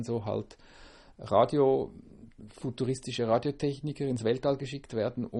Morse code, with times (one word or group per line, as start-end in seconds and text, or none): none